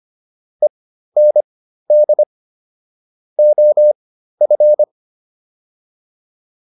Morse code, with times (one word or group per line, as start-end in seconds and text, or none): 0.6-2.2 END
3.4-4.8 OF